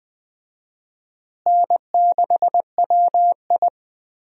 Morse code, heard as a keyed tone, about 20 wpm, 700 Hz. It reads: N6WI
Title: USA Call Signs - 4 Characters - Repeated 3x - 20wpm